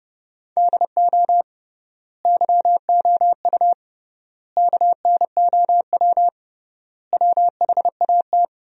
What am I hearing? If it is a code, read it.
Morse code, heard as a keyed tone, about 30 wpm, 700 Hz. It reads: DO YOU KNOW WHAT
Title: 3x Word-Spacing - Sentences from Top 100 Words - 30wpm